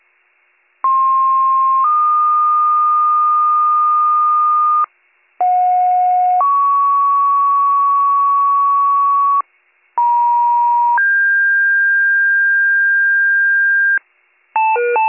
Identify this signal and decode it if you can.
Two-tone page, heard as a one-second tone, then a three-second tone, about 1060 then 1220 Hz, about 730 then 1120 Hz, about 950 then 1600 Hz. It is heard from 0.8 to 14.0 s.